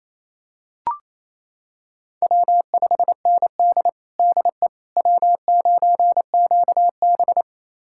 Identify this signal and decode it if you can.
Morse code, heard as a keyed tone, about 28 wpm, 700 Hz. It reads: W5ND DE W9QB
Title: USA Callsign DE Callsign - 4 Characters - 28wpm